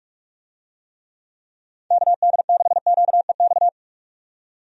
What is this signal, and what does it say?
Morse code, heard as a keyed tone, about 46 wpm, 700 Hz. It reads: KD6QEX
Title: USA Call Signs - 6 Characters - 45wpm